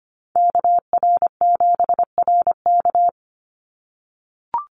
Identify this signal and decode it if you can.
Morse code, heard as a keyed tone, about 25 wpm, 700 Hz. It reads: KR7RK
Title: NA SOTA Activator Callsigns 25wpm